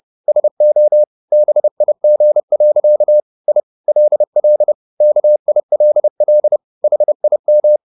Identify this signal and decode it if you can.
Morse code, heard as a keyed tone, about 30 wpm, 600 Hz. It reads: SO BIG. I LL KILL HIM
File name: the old man and the sea / the old man and the sea 1 30wpm